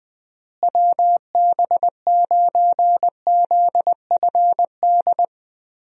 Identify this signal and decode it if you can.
Morse code, heard as a keyed tone, about 21 wpm, 700 Hz. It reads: WB9ZFD
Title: USA Call Signs - 6 Characters - Repeated 2x - 20wpm